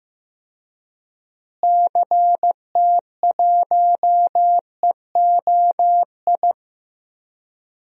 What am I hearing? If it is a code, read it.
Morse code, heard as a keyed tone, about 15 wpm, 700 Hz. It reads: CT1EOI
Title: DXCC-100 Call Signs - 6 Characters - Repeated 3x - 15wpm